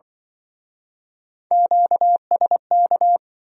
Morse code, heard as a keyed tone, about 26 wpm, 700 Hz. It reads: QSK